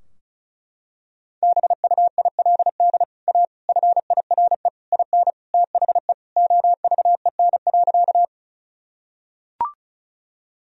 Morse code, heard as a keyed tone, about 36 wpm, 700 Hz. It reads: BUILD A FIRE IN THE OVEN.
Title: Sentences from Top 1000 Words - Review Part 4 - 35wpm